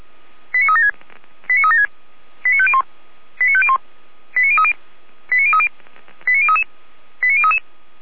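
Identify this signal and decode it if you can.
Five-tone selective call (ZVEI), heard as quick runs of five tones, tones of about 70 ms. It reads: 89376 89376 89501 89501 89030 89030 89033 89033
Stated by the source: ZVEI five-tone call received on an SDR, from the feuerwehr-orlamuende.org archive.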